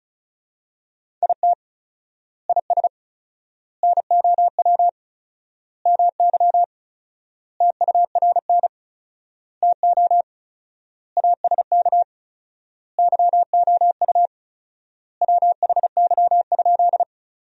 Morse code, heard as a keyed tone, about 34 wpm, 700 Hz. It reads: IT IS NOW MY TURN TO ASK YOU WHY?
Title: 4x Word-Spacing - Sentences from Top 200 Words - 35wpm